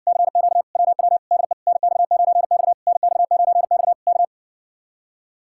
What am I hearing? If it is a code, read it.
Morse code, heard as a keyed tone, about 70 wpm, 700 Hz. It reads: CQ WW DE NX9X NX9X K